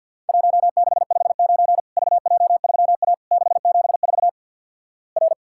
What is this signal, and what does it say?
Morse code, heard as a keyed tone, about 50 wpm, 700 Hz. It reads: 1659 VJ3A 674 R
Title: Worked All Europe DX Contest - QTC Exchange with Repeats - 50wpm